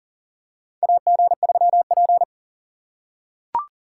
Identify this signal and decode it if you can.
Morse code, heard as a keyed tone, about 40 wpm, 700 Hz. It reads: AG3P
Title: USA Call Signs - 2x Character Spacing - 40wpm